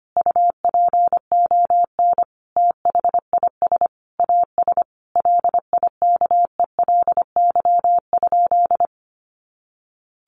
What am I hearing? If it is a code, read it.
Morse code, heard as a keyed tone, about 25 wpm, 700 Hz. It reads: UPON THIS AS LIKELY?
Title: Sentences from Top 1100 Words 25wpm